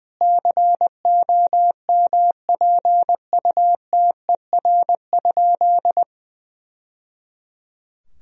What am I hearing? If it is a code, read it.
Morse code, heard as a keyed tone, about 20 wpm, 700 Hz. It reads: COMPUTER?
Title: Sentences from Top 1000 Words - Review Part 3 - 20wpm